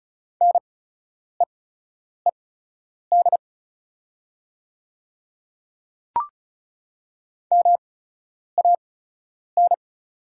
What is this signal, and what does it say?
Morse code, heard as a keyed tone, about 36 wpm, spaced as for 10 wpm, 700 Hz. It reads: NEED MAN
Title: 8x Character Spacing - Top 500 Words - 35wpm